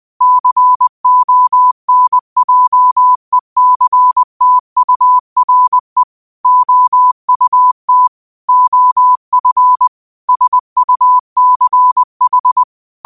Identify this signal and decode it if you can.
Morse code, heard as a keyed tone, about 20 wpm, 1000 Hz. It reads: CONJECTURE OUT OF SUCH
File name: qotdfiles / 20wpm